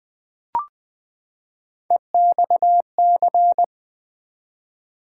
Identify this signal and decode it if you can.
Morse code, heard as a keyed tone, about 20 wpm, 700 Hz. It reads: EXC
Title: Build A Word - 3 Letters At A Time - 6-Letter Words - 20wpm